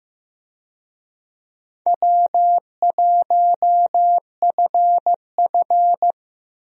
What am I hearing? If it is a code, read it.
Morse code, heard as a keyed tone, about 15 wpm, 700 Hz. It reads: W1FF
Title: USA Call Signs - 4 Characters - 15wpm